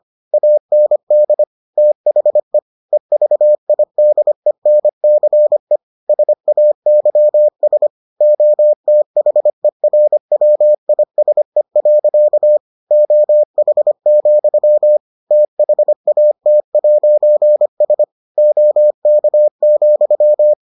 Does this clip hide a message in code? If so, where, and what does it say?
Yes: Morse code from 0.3 to 20.5 s, heard as a keyed tone, about 25 wpm, 600 Hz. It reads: AND THE EVIDENCE SAYS OTHERWISE. OH, THAT'S OK,